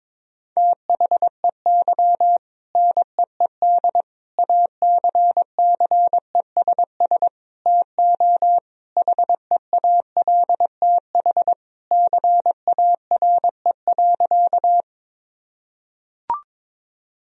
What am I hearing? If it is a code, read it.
Morse code, heard as a keyed tone, about 22 wpm, 700 Hz. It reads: THEY NEED ACCESS TO HEALTH CARE.
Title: Sentences from Top 1000 Words - Review Part 4 - 22wpm